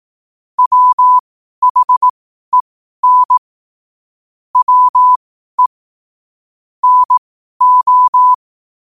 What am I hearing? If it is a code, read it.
Morse code, heard as a keyed tone, about 18 wpm, spaced as for 12 wpm, 1000 Hz. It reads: WHEN WE NO
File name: qotdfiles / 12wpm